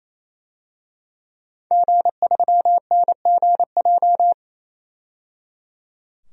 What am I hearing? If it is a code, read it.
Morse code, heard as a keyed tone, about 28 wpm, 700 Hz. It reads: G3NGJ
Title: DXCC-100 Call Signs - 5 Characters - 28wpm